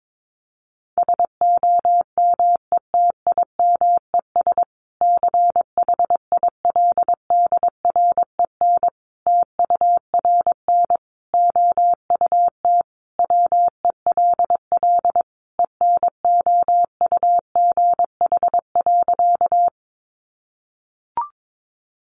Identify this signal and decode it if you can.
Morse code, heard as a keyed tone, about 22 wpm, 700 Hz. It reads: SOMETIMES CHILDREN TURN OUT WELL ENOUGH.
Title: Sentences from Top 600 Words 22wpm